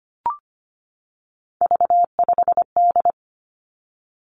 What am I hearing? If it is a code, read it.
Morse code, heard as a keyed tone, about 25 wpm, 700 Hz. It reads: V5D
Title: Three Challenging Letter-Number Combinations 25wpm